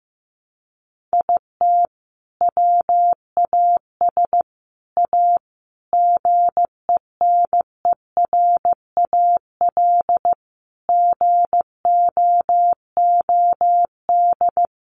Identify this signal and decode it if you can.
Morse code, heard as a keyed tone, about 14 wpm, 700 Hz. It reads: IT WAS A GENERAL GOOD
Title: Sentences from Top 600 Words 15wpm